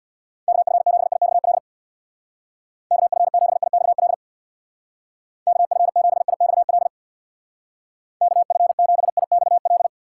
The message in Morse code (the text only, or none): KF7IXL KF7IXL KF7IXL KF7IXL